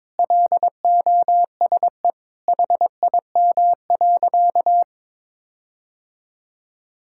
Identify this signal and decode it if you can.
Morse code, heard as a keyed tone, about 21 wpm, 700 Hz. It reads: LOSE HIM.
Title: Sentences from Top 1600 Words 22wpm